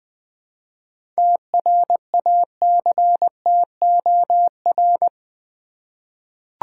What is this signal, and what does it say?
Morse code, heard as a keyed tone, about 20 wpm, 700 Hz. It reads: TRACTOR
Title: Easy Words with Hints - 20wpm